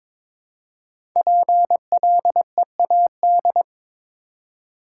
Morse code, heard as a keyed tone, about 22 wpm, 700 Hz. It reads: PLEAD
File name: 5 Letter Words - First 3 Letters Given - 22wpm